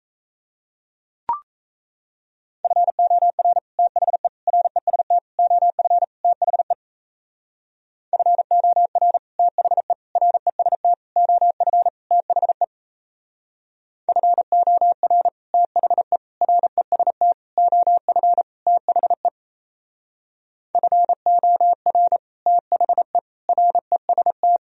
Morse code, heard as a keyed tone, about 38 wpm, 700 Hz. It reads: FOR THE REST OF THE FOR THE REST OF THE FOR THE REST OF THE FOR THE REST